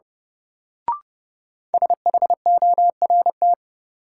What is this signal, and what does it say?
Morse code, heard as a keyed tone, about 30 wpm, 700 Hz. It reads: SHORT
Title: Sets of 5 Words Encore - New Word Drill - 30wpm